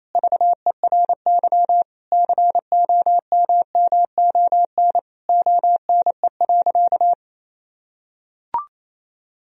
Morse code, heard as a keyed tone, about 28 wpm, 700 Hz. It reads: VERY COMMON ONE.
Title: Sentences from Top 1000 Words - Review Part 4 - 28wpm